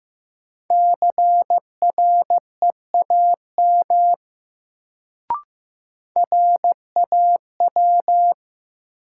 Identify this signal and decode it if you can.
Morse code, heard as a keyed tone, about 15 wpm, 700 Hz. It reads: CREAM RAW
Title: Sentences from Top 1500 Words - New Word Drill - 15wpm